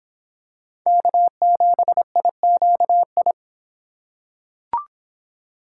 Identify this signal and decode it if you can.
Morse code, heard as a keyed tone, about 23 wpm, 700 Hz. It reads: K7IQI